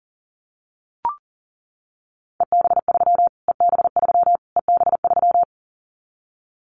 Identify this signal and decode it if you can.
Morse code, heard as a keyed tone, about 40 wpm, 700 Hz. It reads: EB3 EB3 EB3
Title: Callsign Prefixes for Top 25 DXCC - Repeated 3x - 40wpm